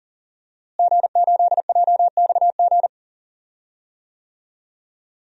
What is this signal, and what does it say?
Morse code, heard as a keyed tone, about 39 wpm, 700 Hz. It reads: G8JXG
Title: DXCC-100 Call Signs - 5 Characters - Repeated 2x - 40wpm